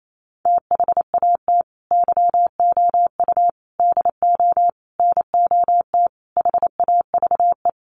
Morse code, heard as a keyed tone, about 28 wpm, 700 Hz. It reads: THAT YOU DO NOT HAVE
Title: Sets of 5 Words Encore 28wpm